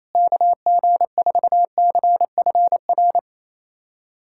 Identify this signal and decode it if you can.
Morse code, heard as a keyed tone, about 28 wpm, 700 Hz. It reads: KG4CFR
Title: USA Callsign DE Callsign - 6 Characters - 28wpm